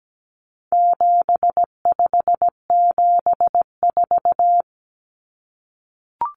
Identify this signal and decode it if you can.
Morse code, heard as a keyed tone, about 17 wpm, 700 Hz. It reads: 7574
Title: Common Cut Numbers - 4-Digit 17wpm